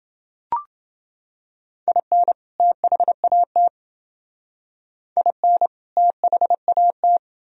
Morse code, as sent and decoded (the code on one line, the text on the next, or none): .. -. / - .... .- - / .. -. / - .... .- -
IN THAT IN THAT